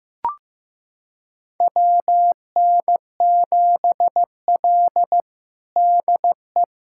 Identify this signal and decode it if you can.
Morse code, heard as a keyed tone, about 15 wpm, 700 Hz. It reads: WN7L DE
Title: USA Callsign DE Callsign - 4 Characters - 15wpm